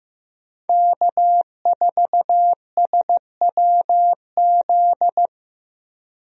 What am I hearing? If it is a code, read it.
Morse code, heard as a keyed tone, about 15 wpm, 700 Hz. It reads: K4SWZ
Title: USA Call Signs - 5 Characters - Repeated 2x - 15wpm